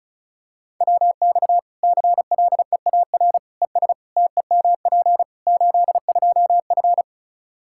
Morse code, heard as a keyed tone, about 36 wpm, 700 Hz. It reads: WX CLEAR ES TEMP 82F
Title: Practice QSO - Exchange 2 - Info - 35wpm